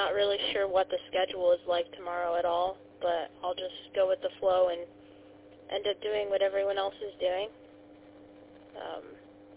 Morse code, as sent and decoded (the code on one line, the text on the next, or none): none